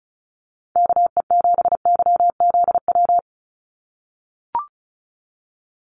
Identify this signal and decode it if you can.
Morse code, heard as a keyed tone, about 34 wpm, 700 Hz. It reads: KE7YZW
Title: USA Call Signs 35wpm